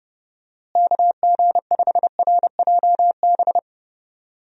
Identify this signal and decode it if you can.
Morse code, heard as a keyed tone, about 30 wpm, 700 Hz. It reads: KG5RJB
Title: USA Call Signs - 6 Characters - Repeated 3x - 30wpm